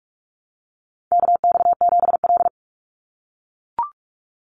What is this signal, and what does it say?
Morse code, heard as a keyed tone, about 44 wpm, 700 Hz. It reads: KX7L
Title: NA SOTA Chaser Callsigns 45wpm